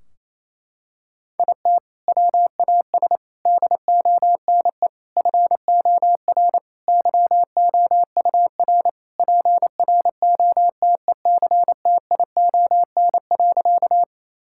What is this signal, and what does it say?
Morse code, heard as a keyed tone, about 28 wpm, 700 Hz. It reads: IT WAS DONE FOR YOUR PROTECTION.
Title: Sentences from Top 1600 Words 28wpm